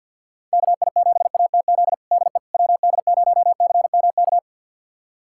Morse code, heard as a keyed tone, about 49 wpm, 700 Hz. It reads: KI7ATZ DE WD0CMK
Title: USA Callsign DE Callsign - 6 Characters - 50wpm